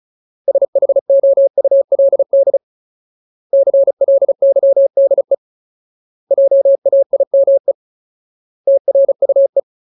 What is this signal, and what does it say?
Morse code, heard as a keyed tone, about 34 wpm, 550 Hz. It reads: SHOULD CLYDE JAIME TRUE
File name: Ultimate Word List / 35WPM Ultimate